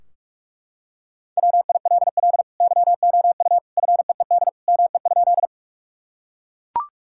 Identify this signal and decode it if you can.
Morse code, heard as a keyed tone, about 41 wpm, 700 Hz. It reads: WILL YOU FEED ME?
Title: Sentences from Top 1600 Words 45wpm